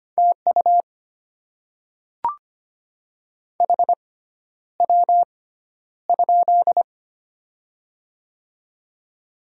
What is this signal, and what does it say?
Morse code, heard as a keyed tone, about 25 wpm, spaced as for 11 wpm, 700 Hz. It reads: TU HW?
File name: Top 20 QSO Elements - 6x Character Spacing - 25wpm